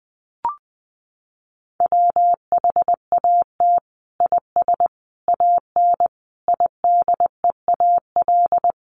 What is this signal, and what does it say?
Morse code, heard as a keyed tone, about 20 wpm, 700 Hz. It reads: WHAT IS AN IDEAL
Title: Sentences from Top 1400 Words 20wpm